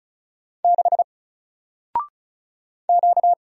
Morse code, heard as a keyed tone, about 35 wpm, 700 Hz. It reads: 6 Q
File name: Single-Letter-Number-35wpm